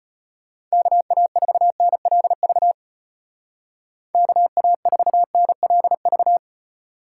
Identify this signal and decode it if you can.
Morse code, heard as a keyed tone, about 34 wpm, 700 Hz. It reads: KA4NLV KA4NLV